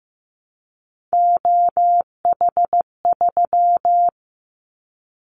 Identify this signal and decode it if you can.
Morse code, heard as a keyed tone, about 15 wpm, 700 Hz. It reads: OH3
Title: Callsign Prefixes for Top 50 DXCC - 15wpm